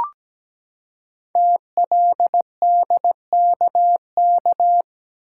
Morse code, heard as a keyed tone, about 17 wpm, 700 Hz. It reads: TLDKK
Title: Code Groups - 5 Letters - 17wpm